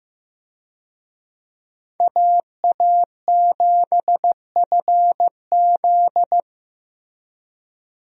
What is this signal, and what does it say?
Morse code, heard as a keyed tone, about 15 wpm, 700 Hz. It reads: AA7FZ
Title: USA Call Signs - 5 Characters - Repeated 2x - 15wpm